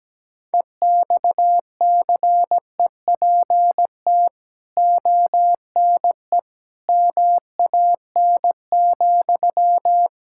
Morse code, heard as a keyed tone, about 17 wpm, 700 Hz. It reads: EXCEPT ONE MAN,
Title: Sentences from Top 1600 Words 17wpm